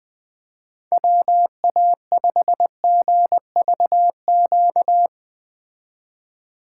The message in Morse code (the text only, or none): WA5GVQ